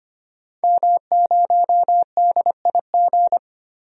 Morse code, heard as a keyed tone, about 25 wpm, 700 Hz. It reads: M0DIG
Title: DXCC-100 Call Signs - 5 Characters - 25wpm